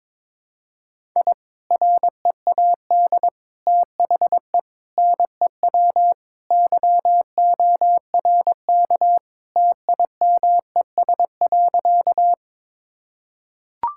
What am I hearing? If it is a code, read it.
Morse code, heard as a keyed tone, about 21 wpm, 700 Hz. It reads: I READ THE NEW YORK TIMES.
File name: practice / 22wpm / Sentences with 3-Word Phrases - 22wpm